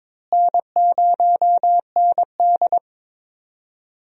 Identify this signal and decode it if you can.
Morse code, heard as a keyed tone, about 23 wpm, 700 Hz. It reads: N0ND
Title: USA Call Signs - 4 Characters - Repeated 3x - 22wpm